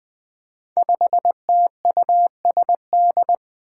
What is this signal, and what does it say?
Morse code, heard as a keyed tone, about 21 wpm, 700 Hz. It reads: 5TUSD
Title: Five Challenging Letter-Number Combinations 20wpm